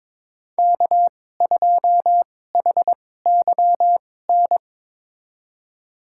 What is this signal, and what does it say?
Morse code, heard as a keyed tone, about 22 wpm, spaced as for 16 wpm, 700 Hz. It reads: K2HYN
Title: USA Call Signs - 2x Character Spacing - 22wpm